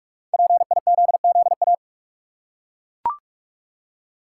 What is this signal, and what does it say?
Morse code, heard as a keyed tone, about 42 wpm, 700 Hz. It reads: PIZZA
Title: Five Letter Words - 45wpm